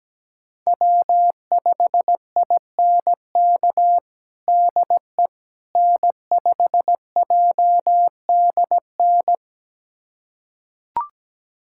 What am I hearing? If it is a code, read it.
Morse code, heard as a keyed tone, about 17 wpm, 700 Hz. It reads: W5INK DE N5JDN
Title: USA Callsign DE Callsign - 5 Characters - 17wpm